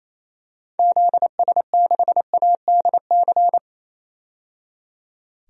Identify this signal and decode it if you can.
Morse code, heard as a keyed tone, about 28 wpm, 700 Hz. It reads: ZS6ADC
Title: DXCC-100 Call Signs - 6 Characters - 28wpm